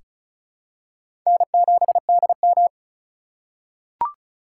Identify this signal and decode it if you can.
Morse code, heard as a keyed tone, about 36 wpm, 700 Hz. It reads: N7DM
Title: USA Call Signs - 4 Characters - 3x Character Spacing - 35wpm